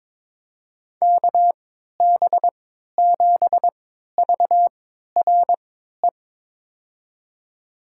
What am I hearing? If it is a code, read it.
Morse code, heard as a keyed tone, about 22 wpm, spaced as for 12 wpm, 700 Hz. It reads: KB7VRE
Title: USA Call Signs - 6 Characters - 3x Character Spacing - 22wpm